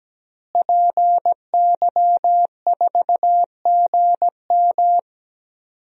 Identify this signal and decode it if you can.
Morse code, heard as a keyed tone, about 17 wpm, 700 Hz. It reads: PY4GM
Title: DXCC-100 Call Signs - 5 Characters - Repeated 3x - 17wpm